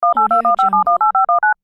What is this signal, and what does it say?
DTMF, heard as telephone touch tones, tones of about 130 ms. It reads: 18215718519
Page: phone-dialing